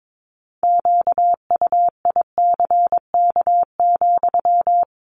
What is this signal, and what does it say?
Morse code, heard as a keyed tone, about 22 wpm, 700 Hz. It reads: QUICK,
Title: Sentences from Top 500 Words 22wpm